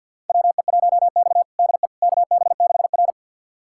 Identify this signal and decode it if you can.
Morse code, heard as a keyed tone, about 47 wpm, 690 Hz. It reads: WE1X DE KB6R
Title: USA Callsign DE Callsign - 4 Characters - 50wpm